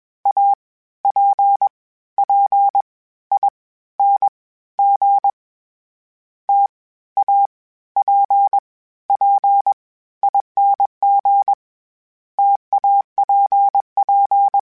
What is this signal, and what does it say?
Morse code, heard as a keyed tone, about 21 wpm, spaced as for 14 wpm, 800 Hz. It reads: APPING TAPPING TAPP
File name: Difficult Combos / Difficult Combos-mp3s / T AOEI @20